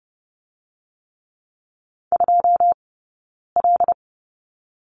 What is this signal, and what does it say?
Morse code, heard as a keyed tone, about 30 wpm, 700 Hz. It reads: 2 L
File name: Lesson 20d - Number 3 - Two Characters 30wpm